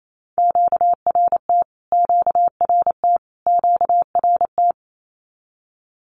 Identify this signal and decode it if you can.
Morse code, heard as a keyed tone, about 27 wpm, 700 Hz. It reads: QRT QRT QRT
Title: Q-Codes Repeated 3x 28wpm